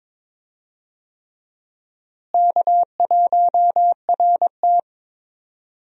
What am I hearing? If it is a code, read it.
Morse code, heard as a keyed tone, about 22 wpm, 700 Hz. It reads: K1RT
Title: USA Call Signs - 4 Characters - Repeated 3x - 22wpm